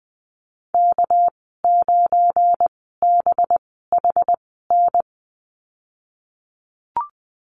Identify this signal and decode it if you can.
Morse code, heard as a keyed tone, about 20 wpm, spaced as for 15 wpm, 700 Hz. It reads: K9BHN
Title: 213 Lesson 34g - Number 8 - Call Signs 20wpm